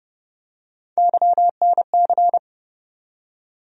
Morse code, heard as a keyed tone, about 30 wpm, 700 Hz. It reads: YNC
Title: USA Call Sign Postfixes - 3 Letters 30wpm